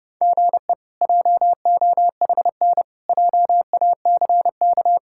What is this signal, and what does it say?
Morse code, heard as a keyed tone, about 32 wpm, 700 Hz. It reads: GE JOHN JACK
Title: K1USN SST QSO Practice - 30wpm